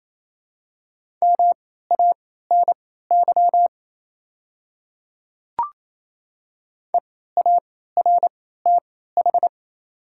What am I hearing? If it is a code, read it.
Morse code, heard as a keyed tone, about 28 wpm, spaced as for 16 wpm, 690 Hz. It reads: MANY EARTH
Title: 3x Character Spacing - Top 200 Words - 28wpm